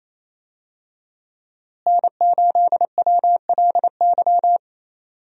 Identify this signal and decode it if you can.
Morse code, heard as a keyed tone, about 28 wpm, 700 Hz. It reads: N8WLY
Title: USA Call Signs - 5 Characters - Repeated 3x - 28wpm